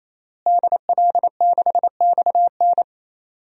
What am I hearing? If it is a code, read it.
Morse code, heard as a keyed tone, about 28 wpm, 700 Hz. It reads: DL6XN
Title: DXCC-100 Call Signs - 5 Characters - Repeated 3x - 28wpm